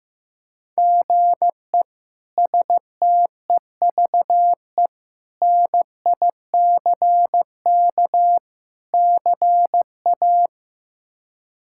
Morse code, heard as a keyed tone, about 15 wpm, 700 Hz. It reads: GE STEVE NICK CA
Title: K1USN SST QSO Practice - 15wpm